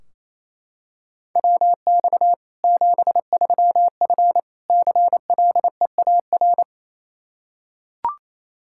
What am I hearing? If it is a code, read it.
Morse code, heard as a keyed tone, about 28 wpm, 700 Hz. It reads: WX 73F CLEAR
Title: Practice QSO - Exchange 2 - Info - 28wpm